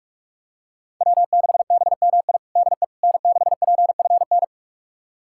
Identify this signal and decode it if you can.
Morse code, heard as a keyed tone, about 42 wpm, 700 Hz. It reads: W6BMI DE N6PFN
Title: USA Callsign DE Callsign - 5 Characters - 45wpm